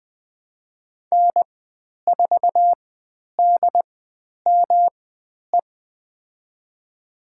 Morse code, heard as a keyed tone, about 20 wpm, spaced as for 10 wpm, 700 Hz. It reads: N4DME
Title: 109 Lesson 18g - Letter m - Call Signs 20wpm